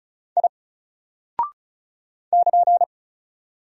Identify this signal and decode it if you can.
Morse code, heard as a keyed tone, about 35 wpm, 700 Hz. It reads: I <KN>
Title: Single Character (Letter:Number:Punctuation:Prosign) 35wpm